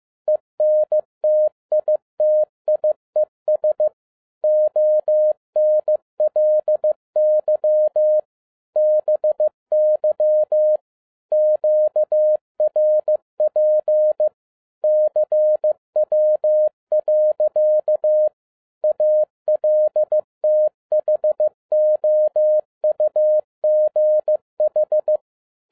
Morse code, heard as a keyed tone, about 15 wpm, 610 Hz. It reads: ENTITIES ONLY BY QRP CW. ALTHOUGH
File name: EXCITING QRP CW OPERATION 15wpm